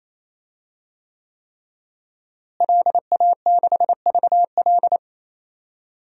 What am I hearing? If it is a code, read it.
Morse code, heard as a keyed tone, about 28 wpm, 700 Hz. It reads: LA6VL
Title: DXCC-100 Call Signs - 5 Characters - Repeated 3x - 28wpm